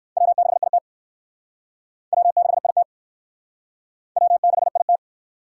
Morse code, heard as a keyed tone, about 60 wpm, 700 Hz. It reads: W6IT W6IT W6IT